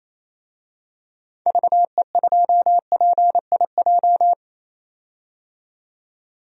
Morse code, heard as a keyed tone, about 28 wpm, 700 Hz. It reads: VE2PIJ
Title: DXCC-100 Call Signs - 6 Characters - 28wpm